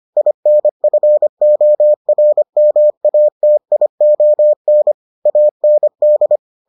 Morse code, heard as a keyed tone, about 25 wpm, 600 Hz. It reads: INFORMATION AND